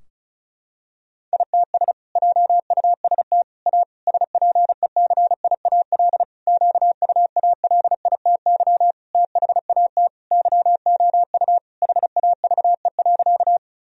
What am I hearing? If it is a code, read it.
Morse code, heard as a keyed tone, about 36 wpm, 700 Hz. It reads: ITS JUST A SPECIAL QUALITY THAT YOU HAVE.
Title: Sentences from Top 1000 Words - Review Part 4 - 35wpm